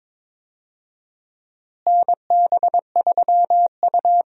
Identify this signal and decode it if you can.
Morse code, heard as a keyed tone, about 22 wpm, 700 Hz. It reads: NB3U
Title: USA Call Signs - 4 Characters - Repeated 2x - 22wpm